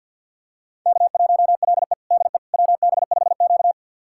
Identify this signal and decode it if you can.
Morse code, heard as a keyed tone, about 52 wpm, 700 Hz. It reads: K1LE DE WB5Q
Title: USA Callsign DE Callsign - 4 Characters - 50wpm